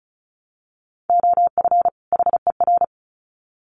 Morse code, heard as a keyed tone, about 34 wpm, 700 Hz. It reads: OF HER
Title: Sets of 2 Words 35wpm